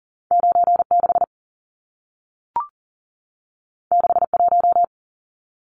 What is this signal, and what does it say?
Morse code, heard as a keyed tone, about 40 wpm, 700 Hz. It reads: 96 61
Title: Numbers - 2-Digit 40wpm